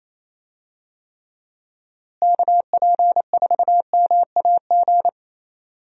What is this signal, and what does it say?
Morse code, heard as a keyed tone, about 28 wpm, 700 Hz. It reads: KP4MAG
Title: USA Call Signs - 6 Characters - Repeated 3x - 28wpm